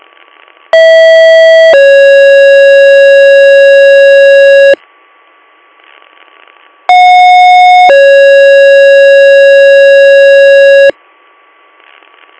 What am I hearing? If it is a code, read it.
Two-tone page, heard as a one-second tone, then a three-second tone, about 650 then 550 Hz, about 730 then 550 Hz, with noise. It is heard from 0.7 to 10.9 s.